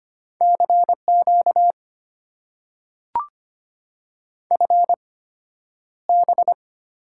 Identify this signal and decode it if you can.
Morse code, heard as a keyed tone, about 25 wpm, spaced as for 11 wpm, 690 Hz. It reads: CQ FB
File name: Top 90 QSO Elements - 8x Character Spacing - 25wpm